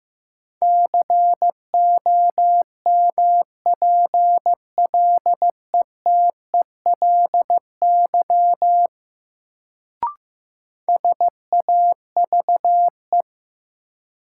Morse code, heard as a keyed tone, about 15 wpm, 700 Hz. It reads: COMPLETELY SAVE
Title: Sentences from Top 1100 Words - New Word Drill - 15wpm